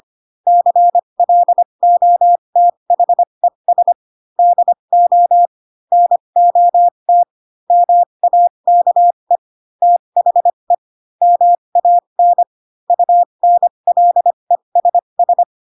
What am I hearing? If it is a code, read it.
Morse code, heard as a keyed tone, about 25 wpm, spaced as for 22 wpm, 700 Hz. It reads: CLOTHES DO NOT MAKE THE MAN UNLESS